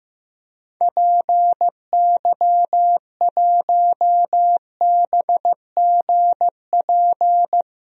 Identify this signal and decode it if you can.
Morse code, heard as a keyed tone, about 15 wpm, 700 Hz. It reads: PY1BGP
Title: DXCC-100 Call Signs - 6 Characters - Repeated 3x - 15wpm